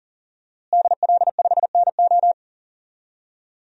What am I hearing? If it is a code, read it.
Morse code, heard as a keyed tone, about 40 wpm, 700 Hz. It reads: DL5NO
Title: DXCC-100 Call Signs - 5 Characters - Repeated 3x - 40wpm